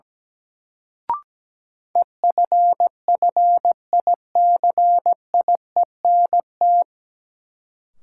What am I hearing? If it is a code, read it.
Morse code, heard as a keyed tone, about 17 wpm, 700 Hz. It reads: EFFICIENT
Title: Sentences from Top 1300 Words - New Word Drill - 17wpm